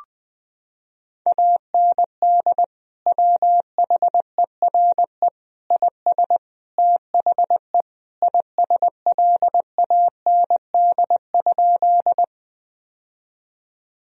Morse code, heard as a keyed tone, about 20 wpm, 700 Hz. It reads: AND WHERE IS THE ISLAND?
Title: Sentences from Top 1000 Words - Review Part 3 - 20wpm